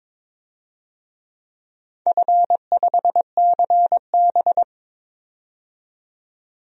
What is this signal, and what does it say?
Morse code, heard as a keyed tone, about 22 wpm, 700 Hz. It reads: F5CB
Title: DXCC-100 Call Signs - 4 Characters - 22wpm